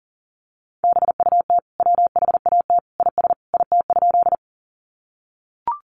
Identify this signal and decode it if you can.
Morse code, heard as a keyed tone, about 40 wpm, 700 Hz. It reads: BUT WHAT IS IT?
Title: Sentences from Top 100 Words 40wpm